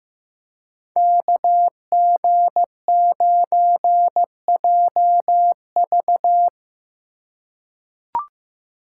Morse code, heard as a keyed tone, about 15 wpm, 700 Hz. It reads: KG9JV
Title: USA Call Signs - 2x Character Spacing - 15wpm